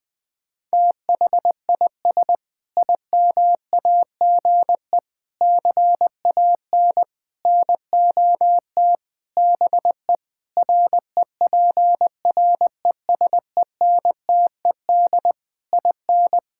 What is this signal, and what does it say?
Morse code, heard as a keyed tone, about 20 wpm, 700 Hz. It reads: THIS IMAGE CAN NOT BE REPRESENTED IN
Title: Sentences with 5-Word Phrases - 20wpm